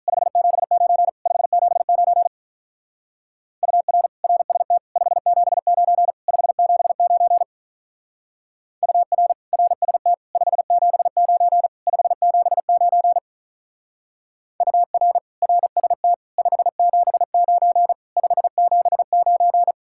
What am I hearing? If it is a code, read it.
Morse code, heard as a keyed tone, about 43 wpm, 700 Hz. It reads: 579 579 UR RST 579 579 UR RST 579 579 UR RST 579 579
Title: Speed-Racing Common QSO Phrases Encore - 35wpm